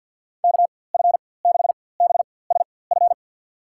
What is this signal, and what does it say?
Morse code, heard as a keyed tone, about 51 wpm, spaced as for 23 wpm, 700 Hz. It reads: KF6BSF
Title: USA Call Signs - 6 Characters - 4x Character Spacing - 50wpm